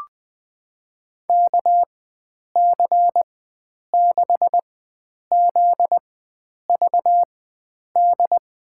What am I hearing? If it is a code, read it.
Morse code, heard as a keyed tone, about 20 wpm, spaced as for 9 wpm, 700 Hz. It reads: KC6ZVD
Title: USA Call Signs - 6 Characters - 4x Character Spacing - 20wpm